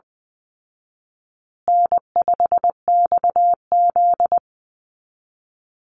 Morse code, heard as a keyed tone, about 20 wpm, 700 Hz. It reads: N5XZ
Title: Easy Call Signs Repeated 3x - 20wpm